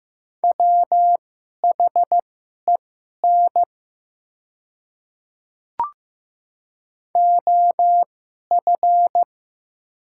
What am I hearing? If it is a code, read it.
Morse code, heard as a keyed tone, about 15 wpm, spaced as for 11 wpm, 690 Hz. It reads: WHEN OF
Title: 2x Character Spacing - Top 100 Words - 15wpm